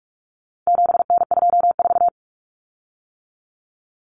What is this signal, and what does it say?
Morse code, heard as a keyed tone, about 44 wpm, 700 Hz. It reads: 7N24
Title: Common Cut Numbers - 4-Digit 45wpm